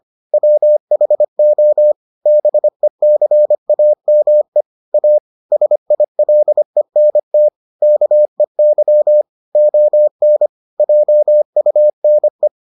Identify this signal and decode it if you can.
Morse code, heard as a keyed tone, about 25 wpm, 600 Hz. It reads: WHO BECAME A SILENT KEY ON JUNE